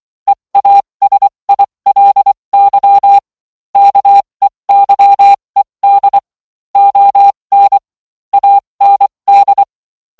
Morse code, heard as a keyed tone, about 24 wpm, spaced as for 20 wpm, 800 Hz. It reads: EASILY KEYED ON AND